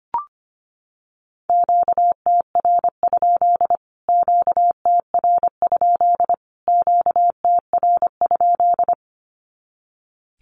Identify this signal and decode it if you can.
Morse code, heard as a keyed tone, about 25 wpm, 700 Hz. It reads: QTR? QTR? QTR?
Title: Q-Codes Repeated 3x 25wpm